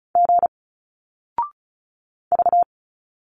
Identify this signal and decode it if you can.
Morse code, heard as a keyed tone, about 34 wpm, 700 Hz. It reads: G V
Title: Single Letters 35wpm